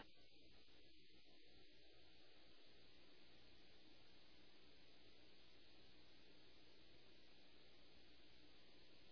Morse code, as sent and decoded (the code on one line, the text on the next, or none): none